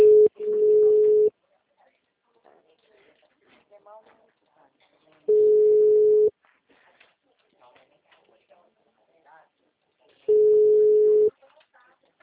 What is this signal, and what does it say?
Ringback tone (Europe, 425 Hz), with speech.